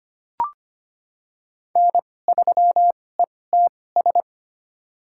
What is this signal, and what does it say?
Morse code, heard as a keyed tone, about 25 wpm, spaced as for 18 wpm, 700 Hz. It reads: N3ETS
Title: USA Call Signs - 5 Characters - 2x Character Spacing - 25wpm